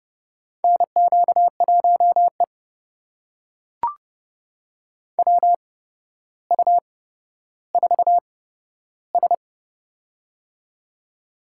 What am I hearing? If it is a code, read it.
Morse code, heard as a keyed tone, about 30 wpm, spaced as for 13 wpm, 700 Hz. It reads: NQ1E WU4S
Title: USA Call Signs - 4 Characters - 8x Character Spacing - 30wpm